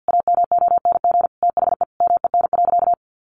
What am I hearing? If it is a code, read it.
Morse code, heard as a keyed tone, about 50 wpm, 700 Hz. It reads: AMONG THE MEN.
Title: Sentences from Top 1400 Words 50wpm